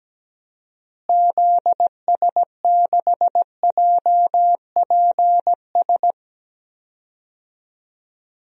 Morse code, heard as a keyed tone, about 17 wpm, 700 Hz. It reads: ZS6JPS
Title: DXCC-100 Call Signs - 6 Characters - Repeated 3x - 17wpm